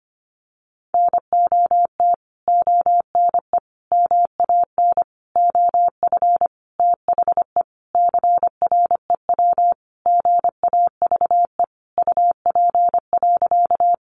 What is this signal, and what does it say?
Morse code, heard as a keyed tone, about 25 wpm, 700 Hz. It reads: NOT ONE MAN OF THE CREW GAVE UP.
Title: Sentences from Top 1500 Words 25wpm